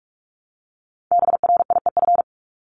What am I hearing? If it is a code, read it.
Morse code, heard as a keyed tone, about 44 wpm, 700 Hz. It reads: BRIEF
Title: Top 1501-1750 Words 45wpm